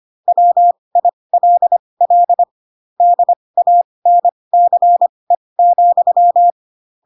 Morse code, heard as a keyed tone, about 25 wpm, spaced as for 20 wpm, 700 Hz. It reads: WILL DANCE,